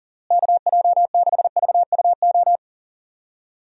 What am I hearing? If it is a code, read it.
Morse code, heard as a keyed tone, about 40 wpm, 700 Hz. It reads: KJ6VUO
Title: USA Call Signs - 6 Characters - Repeated 3x - 40wpm